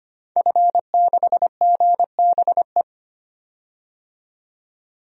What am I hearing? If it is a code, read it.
Morse code, heard as a keyed tone, about 25 wpm, 700 Hz. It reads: F6GBE
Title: DXCC-100 Call Signs - 5 Characters - 25wpm